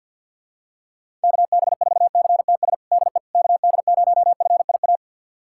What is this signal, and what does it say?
Morse code, heard as a keyed tone, about 52 wpm, 700 Hz. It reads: KB4CTS DE KD0FIA
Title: USA Callsign DE Callsign 50wpm